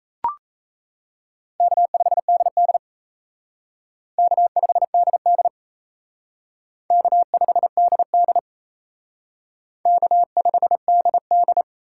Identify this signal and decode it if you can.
Morse code, heard as a keyed tone, about 36 wpm, 700 Hz. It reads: K5DD K5DD K5DD K5DD